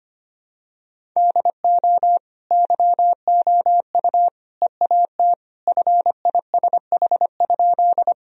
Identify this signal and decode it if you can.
Morse code, heard as a keyed tone, about 25 wpm, 700 Hz. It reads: DO YOU EAT FISH?
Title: Sentences from Top 1000 Words - Review Part 1 - 25wpm